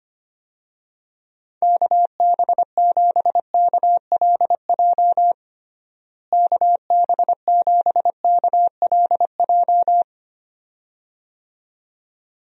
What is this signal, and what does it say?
Morse code, heard as a keyed tone, about 25 wpm, 700 Hz. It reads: KB7KLJ KB7KLJ
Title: USA Call Signs - 6 Characters - Repeated 2x - 25wpm